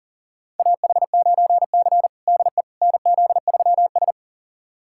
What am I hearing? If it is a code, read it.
Morse code, heard as a keyed tone, about 40 wpm, 700 Hz. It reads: AH9C DE NZ3S
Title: USA Callsign DE Callsign - 4 Characters - 40wpm